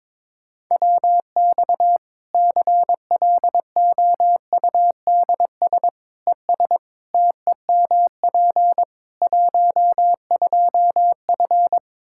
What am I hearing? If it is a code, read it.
Morse code, heard as a keyed tone, about 22 wpm, 700 Hz. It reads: WX CLOUDS ES TEMP 12F
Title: K1USN SST QSO Practice 22wpm